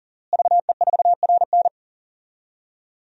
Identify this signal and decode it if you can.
Morse code, heard as a keyed tone, about 40 wpm, 700 Hz. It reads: VE4RN
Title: DXCC-100 Call Signs - 5 Characters - Repeated 3x - 40wpm